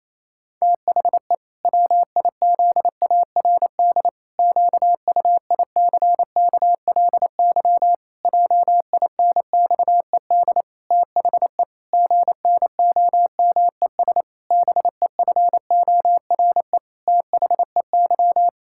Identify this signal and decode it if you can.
Morse code, heard as a keyed tone, about 28 wpm, 700 Hz. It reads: THE WIZARD QUICKLY JINXED THE GNOMES BEFORE THEY
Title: Pangram Challenge v2 - 28wpm